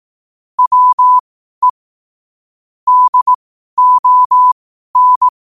Morse code, heard as a keyed tone, about 18 wpm, spaced as for 12 wpm, 1000 Hz. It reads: WE DON
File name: qotdfiles / 12wpm